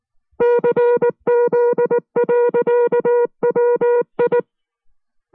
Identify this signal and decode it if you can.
Morse code, heard as a keyed tone, about 19 wpm, 480 Hz. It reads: CZ.WI